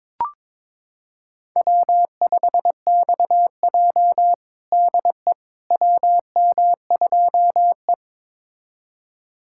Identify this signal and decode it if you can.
Morse code, heard as a keyed tone, about 22 wpm, 700 Hz. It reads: W5XJ DE WM2E
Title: USA Callsign DE Callsign - 4 Characters - 22wpm